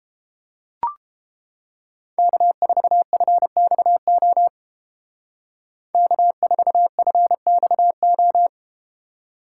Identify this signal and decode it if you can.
Morse code, heard as a keyed tone, about 31 wpm, 700 Hz. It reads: K4FXO K4FXO